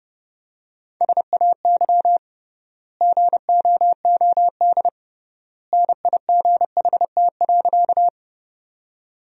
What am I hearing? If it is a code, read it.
Morse code, heard as a keyed tone, about 30 wpm, 700 Hz. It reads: SAY GOOD NIGHT.
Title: 3x Word-Spacing - Sentences from Top 300 Words - 30wpm